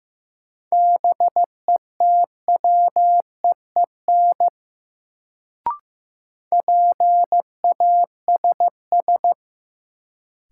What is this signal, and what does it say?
Morse code, heard as a keyed tone, about 15 wpm, 700 Hz. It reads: BETWEEN PASS
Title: Sentences from Top 400 Words - New Word Drill - 15wpm